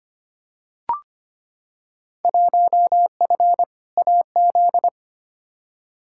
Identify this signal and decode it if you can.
Morse code, heard as a keyed tone, about 25 wpm, 700 Hz. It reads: 1F AZ
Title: Field Day Exchanges - 25wpm